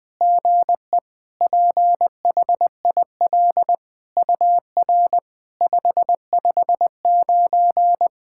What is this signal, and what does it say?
Morse code, heard as a keyed tone, about 20 wpm, 690 Hz. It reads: GE PHIL UR 559
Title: SOTA QSO Practice 20wpm